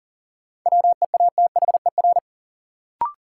Morse code, heard as a keyed tone, about 38 wpm, 700 Hz. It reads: WEATHER
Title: Sentences from Top 800 Words - New Word Drill - 40wpm